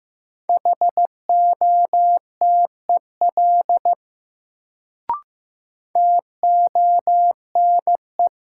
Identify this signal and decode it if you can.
Morse code, heard as a keyed tone, about 15 wpm, 700 Hz. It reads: HOTEL TONE
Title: Sentences from Top 1600 Words - New Word Drill Alt - 15wpm